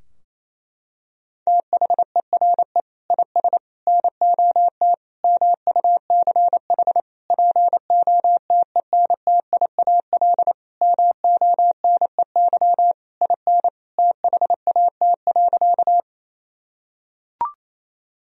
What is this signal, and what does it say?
Morse code, heard as a keyed tone, about 28 wpm, 700 Hz. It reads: THERE IS NOT MUCH POTENTIAL MONEY IN THAT.
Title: Sentences from Top 1000 Words - Review Part 4 - 28wpm